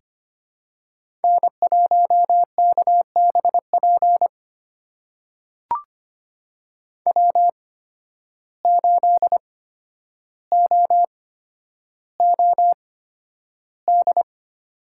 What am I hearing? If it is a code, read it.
Morse code, heard as a keyed tone, about 25 wpm, spaced as for 11 wpm, 700 Hz. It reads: N1KBP W8OOD
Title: USA Call Signs - 5 Characters - 8x Character Spacing - 25wpm